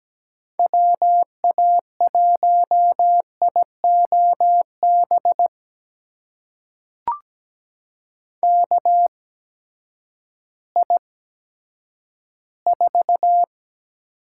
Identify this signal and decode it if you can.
Morse code, heard as a keyed tone, about 17 wpm, spaced as for 10 wpm, 700 Hz. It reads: WA1IOB KI4
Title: USA Call Signs - 6 Characters - 8x Character Spacing - 17wpm